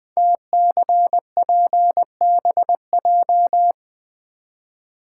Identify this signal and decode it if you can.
Morse code, heard as a keyed tone, about 20 wpm, 700 Hz. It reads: TCPBJ
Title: Code Groups - 5 Letters - 20wpm